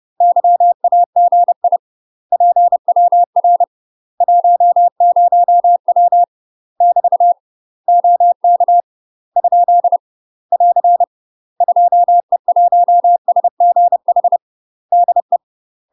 Morse code, heard as a keyed tone, about 30 wpm, 700 Hz. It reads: YAGI PWR 10W <BT> OK ? <AR> 2E1SGH DE
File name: UK MASTER QSOs TRAINING / 30WPM / UK MASTER QSOs 0035